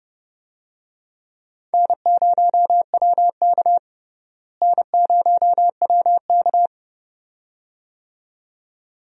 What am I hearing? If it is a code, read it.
Morse code, heard as a keyed tone, about 30 wpm, 700 Hz. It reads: N0WK N0WK
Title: USA Call Signs - 4 Characters - Repeated 2x - 30wpm